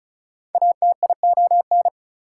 Morse code, heard as a keyed tone, about 34 wpm, 700 Hz. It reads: ATION
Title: Prefixes and Suffixes 35wpm